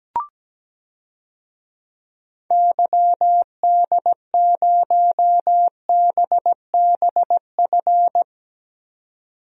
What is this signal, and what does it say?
Morse code, heard as a keyed tone, about 17 wpm, 700 Hz. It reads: YD0BBF
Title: DXCC-100 Call Signs - 6 Characters - 17wpm